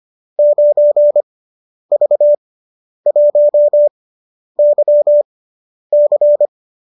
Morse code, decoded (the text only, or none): 9V1YC